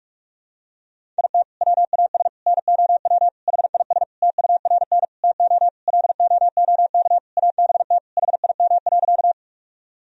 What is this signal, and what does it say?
Morse code, heard as a keyed tone, about 41 wpm, 700 Hz. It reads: IT WAS NOW HIS TURN TO LOOK ABT HIM.
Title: Sentences from Top 200 Words and Abbreviations 45wpm